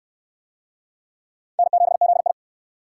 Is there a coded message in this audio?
Yes: Morse code from 1.6 to 2.3 s, heard as a keyed tone, about 68 wpm, 700 Hz. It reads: N7ZI